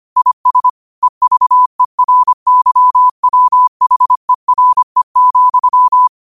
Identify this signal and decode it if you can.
Morse code, heard as a keyed tone, about 25 wpm, 1000 Hz. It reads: IS EVERYWHERE,